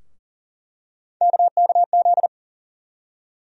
Morse code, heard as a keyed tone, about 40 wpm, 700 Hz. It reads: KKZ
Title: USA Call Sign Postfixes - 3 Letters 40wpm